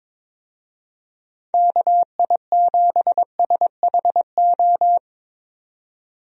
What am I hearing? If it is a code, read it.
Morse code, heard as a keyed tone, about 22 wpm, 700 Hz. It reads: KI7SHO
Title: USA Call Signs - 6 Characters - 22wpm